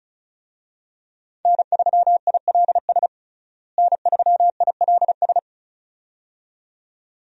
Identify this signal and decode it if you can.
Morse code, heard as a keyed tone, about 34 wpm, 700 Hz. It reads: N3ILS N3ILS
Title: USA Call Signs - 5 Characters - Repeated 2x - 35wpm